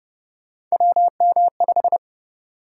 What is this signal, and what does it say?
Morse code, heard as a keyed tone, about 32 wpm, 700 Hz. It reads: WM5